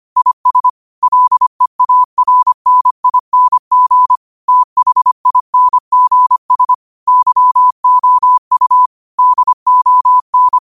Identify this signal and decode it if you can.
Morse code, heard as a keyed tone, about 25 wpm, 1000 Hz. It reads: IS LEARNING THINGS YOU DON